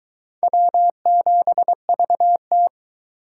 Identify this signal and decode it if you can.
Morse code, heard as a keyed tone, about 23 wpm, 700 Hz. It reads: W7VT